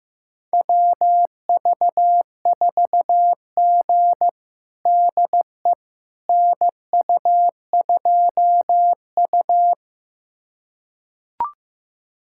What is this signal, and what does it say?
Morse code, heard as a keyed tone, about 15 wpm, 700 Hz. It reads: WV4G DE NU2U
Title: USA Callsign DE Callsign - 4 Characters - 15wpm